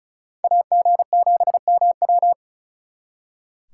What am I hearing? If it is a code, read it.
Morse code, heard as a keyed tone, about 43 wpm, 700 Hz. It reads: AG7MW
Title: Speed Racing - USA 2x2 Call Signs 35wpm